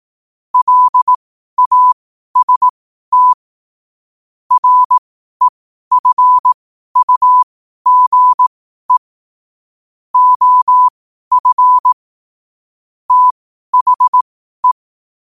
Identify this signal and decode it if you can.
Morse code, heard as a keyed tone, about 18 wpm, spaced as for 12 wpm, 1000 Hz. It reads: LAST REFUGE OF THE
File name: qotdfiles / 12wpm